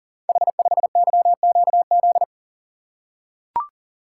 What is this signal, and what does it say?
Morse code, heard as a keyed tone, about 43 wpm, 700 Hz. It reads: H5YQZ
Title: Code Groups - 5 Letter:Numbers - 40wpm